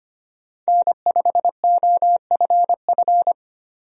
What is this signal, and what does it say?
Morse code, heard as a keyed tone, about 25 wpm, 700 Hz. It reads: N5OFF
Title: USA Call Signs - 5 Characters - 25wpm